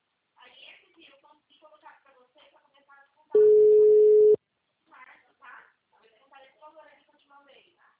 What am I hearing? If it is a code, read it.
Ringback tone (Europe, 425 Hz), with speech.